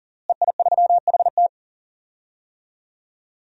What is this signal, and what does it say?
Morse code, heard as a keyed tone, about 40 wpm, 700 Hz. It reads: EI3HT
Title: DXCC-100 Call Signs - 5 Characters - Repeated 3x - 40wpm